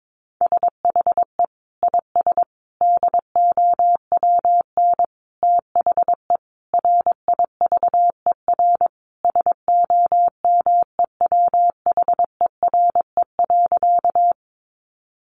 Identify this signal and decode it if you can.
Morse code, heard as a keyed tone, about 22 wpm, 700 Hz. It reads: SHE IS DOWN THE RIVER SOMEWHERE.
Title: Sentences from Top 1400 Words 22wpm